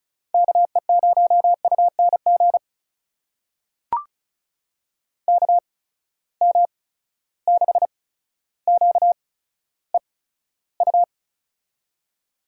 Morse code, heard as a keyed tone, about 36 wpm, spaced as for 15 wpm, 700 Hz. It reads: KE0UNG KM6QEU
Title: USA Call Signs - 6 Characters - 8x Character Spacing - 35wpm